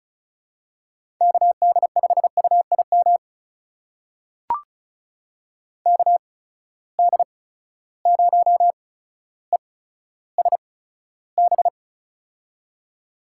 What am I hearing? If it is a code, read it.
Morse code, heard as a keyed tone, about 35 wpm, spaced as for 15 wpm, 700 Hz. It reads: KD5UIM KD0ESB
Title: USA Call Signs - 6 Characters - 8x Character Spacing - 35wpm